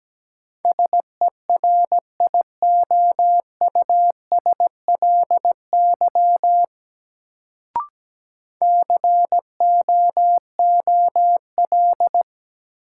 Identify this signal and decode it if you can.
Morse code, heard as a keyed tone, about 17 wpm, 700 Hz. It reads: SERIOUSLY COOL
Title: Sentences from Top 1200 Words - New Word Drill - 17wpm